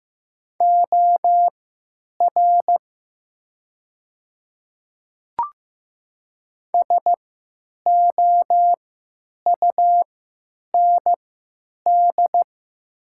Morse code, heard as a keyed tone, about 15 wpm, spaced as for 9 wpm, 700 Hz. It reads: OR SOUND
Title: 3x Character Spacing - Top 100 Words - 15wpm